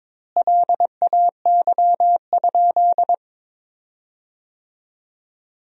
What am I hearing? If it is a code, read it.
Morse code, heard as a keyed tone, about 22 wpm, 700 Hz. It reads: LAY?
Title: 3x Word-Spacing - Sentences from Top 400 Words - 22wpm